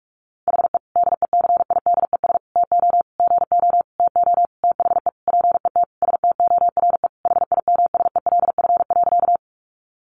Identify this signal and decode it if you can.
Morse code, heard as a keyed tone, about 44 wpm, 700 Hz. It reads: HE DECIDES TO GO TO THE PET STORE HIMSELF.
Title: Additional Easy Phrases 45wpm